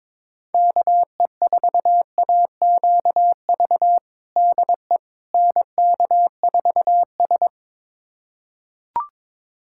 Morse code, heard as a keyed tone, about 22 wpm, 700 Hz. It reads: KE4AQV DE NK4S
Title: USA Callsign DE Callsign 22wpm